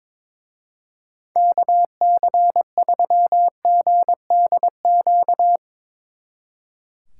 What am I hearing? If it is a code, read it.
Morse code, heard as a keyed tone, about 24 wpm, 700 Hz. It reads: KC3GDQ
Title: Speed Racing - USA 2x3 Call Signs 22wpm